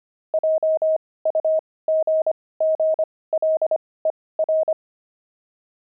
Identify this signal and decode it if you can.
Morse code, heard as a keyed tone, about 25 wpm, spaced as for 18 wpm, 630 Hz. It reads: JUGGLER
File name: pangram-fs-18-cs-25-dexterity